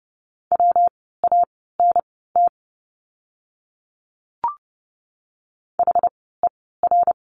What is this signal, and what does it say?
Morse code, heard as a keyed tone, about 30 wpm, spaced as for 17 wpm, 700 Hz. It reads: WANT HER
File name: course / 30:10wpm / 115 Lesson 19f - Letter w - Words 30wpm